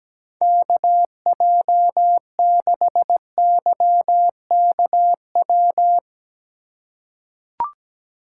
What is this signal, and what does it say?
Morse code, heard as a keyed tone, about 17 wpm, 700 Hz. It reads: KJ6YKW
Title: USA Call Signs - 6 Characters - 3x Character Spacing - 17wpm